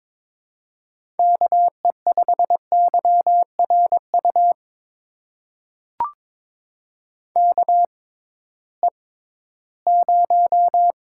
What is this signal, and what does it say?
Morse code, heard as a keyed tone, about 22 wpm, spaced as for 14 wpm, 700 Hz. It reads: KE5YRU KE0
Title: USA Call Signs - 6x Character Spacing - 22wpm